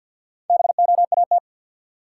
Morse code, heard as a keyed tone, about 53 wpm, 700 Hz. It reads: BOAT
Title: Sentences from Top 500 Words - New Word Drill - 50wpm